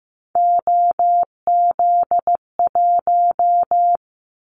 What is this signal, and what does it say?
Morse code, heard as a keyed tone, about 15 wpm, 700 Hz. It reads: OZ1
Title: Callsign Prefixes for Top 100 DXCC - 15wpm